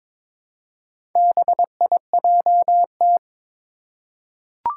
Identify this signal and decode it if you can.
Morse code, heard as a keyed tone, about 22 wpm, 700 Hz. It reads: BIJT
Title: Four Challenging Letter-Number Combinations 22wpm